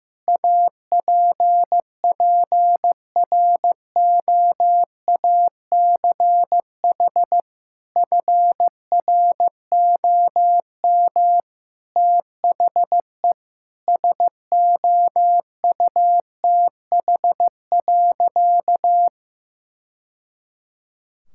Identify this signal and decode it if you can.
Morse code, heard as a keyed tone, about 15 wpm, 700 Hz. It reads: APPROACH FROM THE SOUTH.
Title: Sentences from Top 1000 Words - Review Part 6 - 15wpm